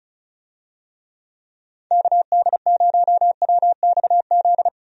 Morse code, heard as a keyed tone, about 36 wpm, 700 Hz. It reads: KD0WXZ
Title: USA Call Signs - 6 Characters - 35wpm